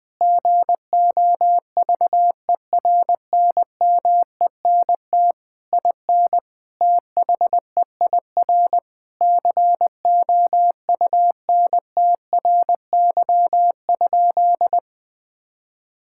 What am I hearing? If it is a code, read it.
Morse code, heard as a keyed tone, about 20 wpm, 700 Hz. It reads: GOVERNMENT IN THEIR COUNTRY?
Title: Sentences from Top 1000 Words - Review Part 5 - 20wpm